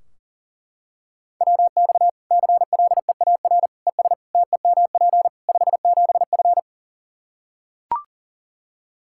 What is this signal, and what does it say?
Morse code, heard as a keyed tone, about 40 wpm, 700 Hz. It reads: WX CLEAR ES TEMP 57F
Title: Practice QSO - Exchange 2 - Info - 40wpm